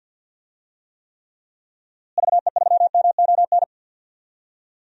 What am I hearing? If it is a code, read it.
Morse code, heard as a keyed tone, about 52 wpm, 700 Hz. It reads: VE3MON
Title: DXCC-100 Call Signs - 6 Characters - Repeated 3x - 50wpm